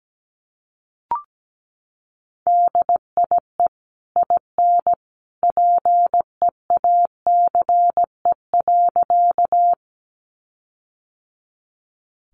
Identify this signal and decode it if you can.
Morse code, heard as a keyed tone, about 17 wpm, 700 Hz. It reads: DIE IN PEACE.
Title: Sentences from Top 1500 Words 17wpm